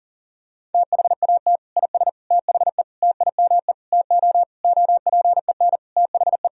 Morse code, heard as a keyed tone, about 37 wpm, 700 Hz. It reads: THAT IS THE TIME TO OPEN THE
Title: Sentences from Top 1000 Words - Review Part 1 - 40wpm